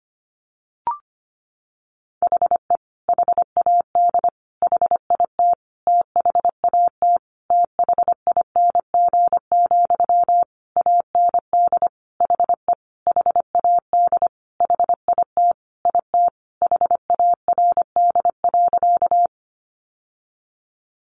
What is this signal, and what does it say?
Morse code, heard as a keyed tone, about 25 wpm, 700 Hz. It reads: HE HAD HIT THAT THING, AND HE HAD HIT IT HARD.
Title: Sentences from Top 1200 Words 25wpm